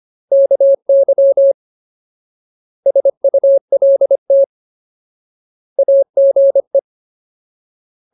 Morse code, heard as a keyed tone, about 25 wpm, 550 Hz. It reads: KY SULT AGE
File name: Ultimate Word List / 25WPM Ultimate